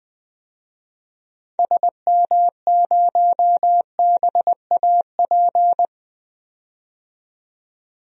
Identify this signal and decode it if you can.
Morse code, heard as a keyed tone, about 20 wpm, 700 Hz. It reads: SM0BAP
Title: DXCC-100 Call Signs - 6 Characters - 20wpm